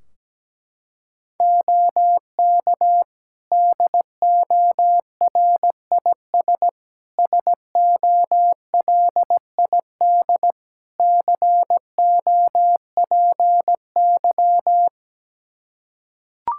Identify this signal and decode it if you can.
Morse code, heard as a keyed tone, about 17 wpm, 700 Hz. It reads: OK DORIS SOLID COPY
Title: Practice QSO - Exchange 3 - More Info - 17wpm